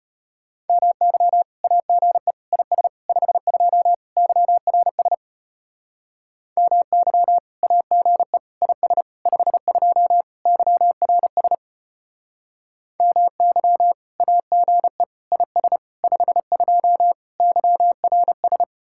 Morse code, heard as a keyed tone, about 31 wpm, 700 Hz. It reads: MY AGE IS 52 YRS MY AGE IS 52 YRS MY AGE IS 52 YRS